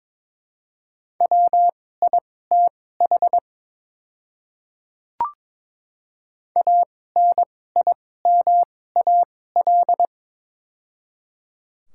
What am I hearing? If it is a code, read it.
Morse code, heard as a keyed tone, about 22 wpm, spaced as for 16 wpm, 700 Hz. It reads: WITH ANIMAL
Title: 2x Character Spacing - Top 200 Words - 22wpm